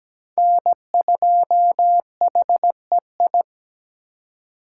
Morse code, heard as a keyed tone, about 20 wpm, 700 Hz. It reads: N2HEI
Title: Speed Racing - USA 1x3 Call Signs 17wpm